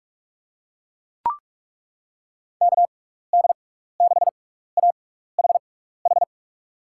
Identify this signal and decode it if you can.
Morse code, heard as a keyed tone, about 42 wpm, spaced as for 15 wpm, 700 Hz. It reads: KD6AHH
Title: USA Call Signs - 6 Characters - 6x Character Spacing - 45wpm